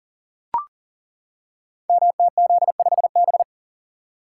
Morse code, heard as a keyed tone, about 37 wpm, 700 Hz. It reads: MTZ5B
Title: Code Groups - 5 Letter:Numbers - 40wpm